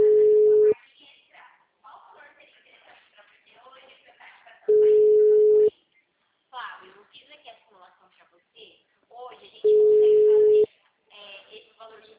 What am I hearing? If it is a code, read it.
Ringback tone (Europe, 425 Hz), with speech.
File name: out- -2008- - - .80